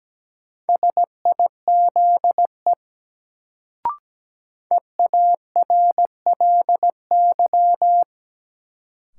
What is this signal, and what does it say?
Morse code, heard as a keyed tone, about 17 wpm, 700 Hz. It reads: SIZE EARLY
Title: Sets of 3 Words - New Word Drill - 17wpm